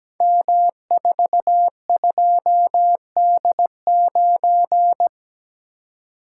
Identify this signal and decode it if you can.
Morse code, heard as a keyed tone, about 17 wpm, 690 Hz. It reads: M42D9
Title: Code Groups - 5 Letter:Numbers - 17wpm